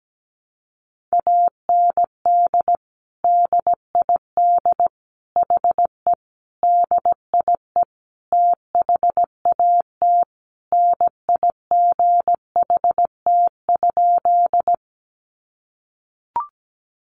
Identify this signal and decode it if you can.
Morse code, heard as a keyed tone, about 17 wpm, 700 Hz. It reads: AND DID HE DIE THAT NIGHT?
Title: Sentences from Top 1300 Words 17wpm